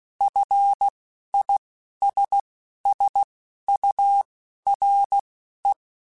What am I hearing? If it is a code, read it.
Morse code, heard as a keyed tone, about 16 wpm, spaced as for 11 wpm, 800 Hz. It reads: FISSURE
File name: General / General-mp3s / ISS @10